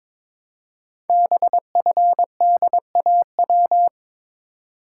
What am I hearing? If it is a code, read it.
Morse code, heard as a keyed tone, about 22 wpm, 700 Hz. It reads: BFDAW
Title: Code Groups - 5 Letters - 22wpm